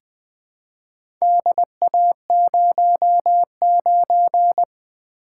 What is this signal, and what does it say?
Morse code, heard as a keyed tone, about 20 wpm, 700 Hz. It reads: DA09
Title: Four Letter-Number Combinations - 20wpm